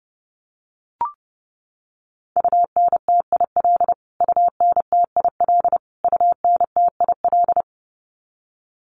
Morse code, heard as a keyed tone, about 30 wpm, 700 Hz. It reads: UNTIL UNTIL UNTIL
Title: Top 300 Words Repeated 30wpm